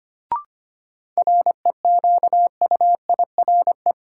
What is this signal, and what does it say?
Morse code, heard as a keyed tone, about 25 wpm, 700 Hz. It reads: REQUIRE
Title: Sentences from Top 900 Words - New Word Drill - 25wpm